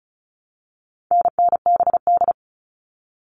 Morse code, heard as a keyed tone, about 34 wpm, 700 Hz. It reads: NNBD
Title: All Cut Numbers - 4-Digit 35wpm